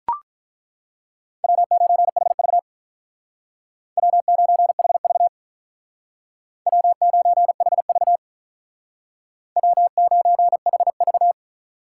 Morse code, heard as a keyed tone, about 43 wpm, 700 Hz. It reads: W9HV W9HV W9HV W9HV
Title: Speed Racing - USA 1x2 Call Signs 35wpm